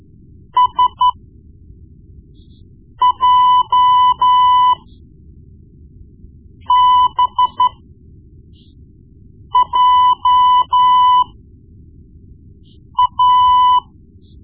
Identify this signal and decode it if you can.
Morse code, heard as a keyed tone, about 12 wpm, spaced as for 4 wpm, 1000 Hz. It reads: SJBJA
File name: Morse Lessons / Morse Lessons / Tape-1B